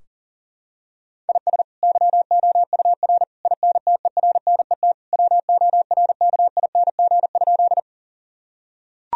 Morse code, heard as a keyed tone, about 40 wpm, 700 Hz. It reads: IS YOUR INTERNET WORKING?
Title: Sentences from Top 1000 Words - Review Part 4 - 40wpm